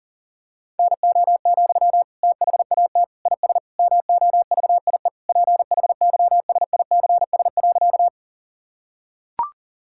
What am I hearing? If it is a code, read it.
Morse code, heard as a keyed tone, about 40 wpm, 700 Hz. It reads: NO, THAT IS MOVIE PHYSICS.
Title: Sentences from Top 1000 Words - Review Part 5 - 40wpm